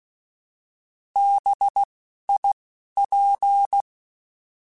Morse code, heard as a keyed tone, about 16 wpm, spaced as for 11 wpm, 800 Hz. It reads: BIP
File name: Alphabetical / Alphabetical mp3s / Bs @10